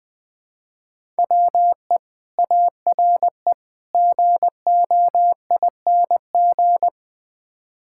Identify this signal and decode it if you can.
Morse code, heard as a keyed tone, about 20 wpm, 700 Hz. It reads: WE ARE GOING
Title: Build A Phrase - 4-Word Phrases - 20wpm